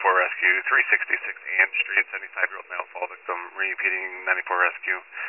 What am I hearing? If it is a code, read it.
Emergency Alert System (SAME) tones, heard as harsh buzzing bursts, with speech. It is heard from 0.4 to 4.4 s.
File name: pager / 94 RESCUE OFC / wcalert